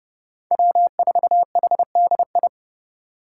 Morse code, heard as a keyed tone, about 26 wpm, 700 Hz. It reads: W4HDI